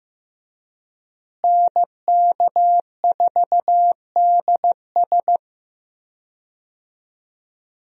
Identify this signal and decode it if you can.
Morse code, heard as a keyed tone, about 15 wpm, 700 Hz. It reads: NK4DS
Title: USA Call Signs - 5 Characters - Repeated 2x - 15wpm